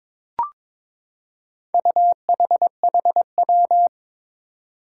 Morse code, heard as a keyed tone, about 22 wpm, 700 Hz. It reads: UHHW
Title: Four Challenging Letter-Number Combinations 22wpm